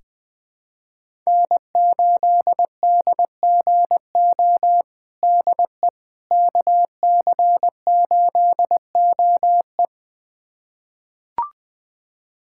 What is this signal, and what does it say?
Morse code, heard as a keyed tone, about 20 wpm, 700 Hz. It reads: N8DGO DE KC8OE
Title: USA Callsign DE Callsign - 5 Characters - 20wpm